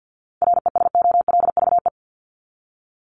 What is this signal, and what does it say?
Morse code, heard as a keyed tone, about 50 wpm, 700 Hz. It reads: RESOLVE
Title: Top 1751-2000 Words 50wpm